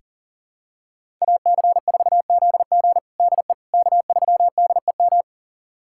Morse code, heard as a keyed tone, about 40 wpm, 700 Hz. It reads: AC4ZG DE K3DEM
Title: USA Callsign DE Callsign - 5 Characters - 40wpm